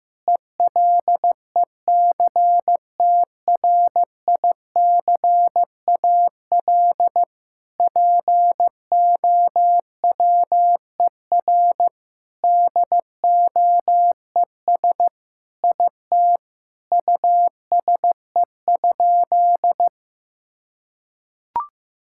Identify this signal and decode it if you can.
Morse code, heard as a keyed tone, about 14 wpm, 700 Hz. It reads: ELECTRICAL POWER DOES IT USE?
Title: Sentences from Top 1000 Words - Review Part 6 - 15wpm